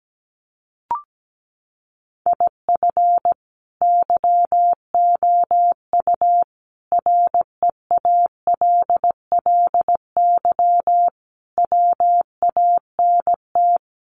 Morse code, heard as a keyed tone, about 17 wpm, 700 Hz. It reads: IF YOU REALLY WANT
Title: Sets of 5 Words Encore2 17wpm